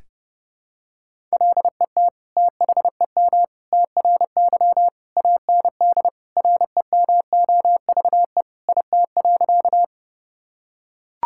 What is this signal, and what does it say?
Morse code, heard as a keyed tone, about 32 wpm, 700 Hz. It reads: LET THEM TRY AND REMOVE IT.
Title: Sentences from Top 1000 Words - Review Part 5 - 30wpm